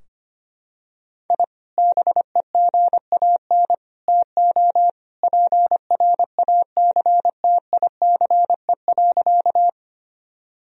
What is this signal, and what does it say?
Morse code, heard as a keyed tone, about 25 wpm, 700 Hz. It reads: I BEGAN TO PRACTICE.
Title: Sentences from Top 1000 Words - Review Part 4 - 25wpm